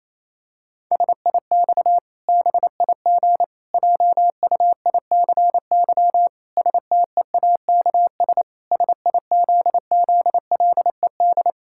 Morse code, heard as a keyed tone, about 28 wpm, 700 Hz. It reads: SIX BIG JUICY STEAKS SIZZLED
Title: Pangram Challenge v2 - 28wpm